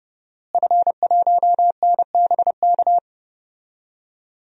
Morse code, heard as a keyed tone, about 32 wpm, 700 Hz. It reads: F1NBK
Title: DXCC-100 Call Signs - 5 Characters - Repeated 2x - 30wpm